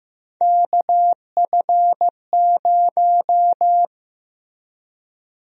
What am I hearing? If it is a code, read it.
Morse code, heard as a keyed tone, about 15 wpm, 700 Hz. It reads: KF0
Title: USA Call Sign Prefixes - Repeated 2x - 15wpm